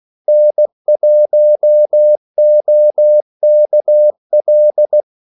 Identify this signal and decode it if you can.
Morse code, heard as a keyed tone, about 16 wpm, 600 Hz. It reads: N1OKL
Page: Kaldesignaler 16 wpm